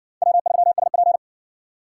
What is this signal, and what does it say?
Morse code, heard as a keyed tone, about 56 wpm, 700 Hz. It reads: W3SP